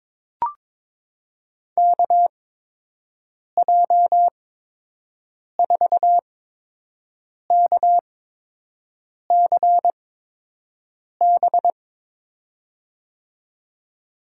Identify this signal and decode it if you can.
Morse code, heard as a keyed tone, about 21 wpm, spaced as for 6 wpm, 700 Hz. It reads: KJ4KCB
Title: USA Call Signs - 6 Characters - 8x Character Spacing - 22wpm